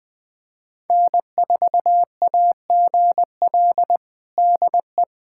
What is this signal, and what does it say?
Morse code, heard as a keyed tone, about 20 wpm, 700 Hz. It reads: N4AGL DE
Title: USA Callsign DE Callsign - 5 Characters - 20wpm